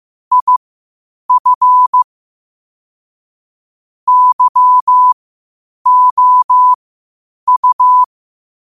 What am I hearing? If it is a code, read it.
Morse code, heard as a keyed tone, about 15 wpm, spaced as for 8 wpm, 1000 Hz. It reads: IF YOU
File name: qotdfiles / 8wpm